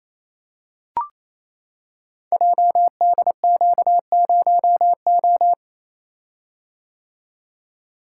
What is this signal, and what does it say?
Morse code, heard as a keyed tone, about 28 wpm, 700 Hz. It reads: JDQ0O
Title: Code Groups - 5 Letter:Numbers - 28wpm